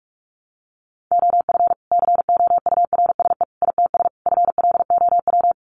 Morse code, heard as a keyed tone, about 44 wpm, 700 Hz. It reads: OF COURSE ITS FLOW
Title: Sentences from Top 1200 Words 45wpm